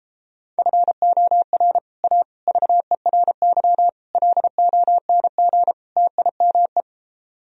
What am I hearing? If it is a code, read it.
Morse code, heard as a keyed tone, about 31 wpm, 700 Hz. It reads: FOR A VERY LONG TIME